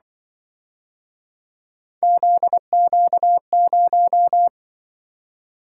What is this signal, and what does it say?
Morse code, heard as a keyed tone, about 23 wpm, 700 Hz. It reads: ZQ0
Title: Speed-Racing - Three Letter-Number Combinations - 20wpm